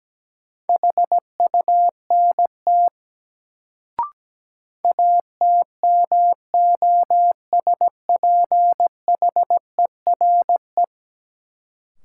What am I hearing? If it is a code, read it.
Morse code, heard as a keyed tone, about 17 wpm, 700 Hz. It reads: HUNT ATMOSPHERE
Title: Sentences from Top 1600 Words - New Word Drill - 17wpm